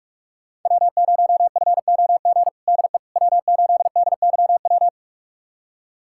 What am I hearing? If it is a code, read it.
Morse code, heard as a keyed tone, about 43 wpm, 700 Hz. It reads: W0FOG DE W8DYW
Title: USA Callsign DE Callsign - 5 Characters - 45wpm